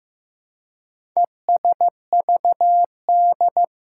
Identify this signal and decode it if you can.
Morse code, heard as a keyed tone, about 15 wpm, 700 Hz. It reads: ESVD
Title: Four Challenging Letter-Number Combinations 15wpm